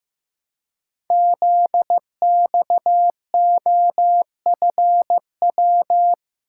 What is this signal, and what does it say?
Morse code, heard as a keyed tone, about 15 wpm, 700 Hz. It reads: ZXOFW
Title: Code Groups - 5 Letter:Numbers - 15wpm